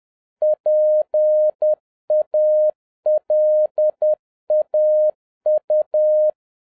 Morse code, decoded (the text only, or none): PALAU